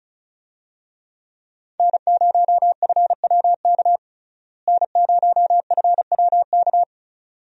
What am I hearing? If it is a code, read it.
Morse code, heard as a keyed tone, about 34 wpm, 700 Hz. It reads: N0FWK N0FWK
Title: USA Call Signs - 5 Characters - Repeated 2x - 35wpm